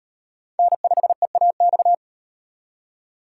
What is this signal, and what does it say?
Morse code, heard as a keyed tone, about 38 wpm, 700 Hz. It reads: N5EAX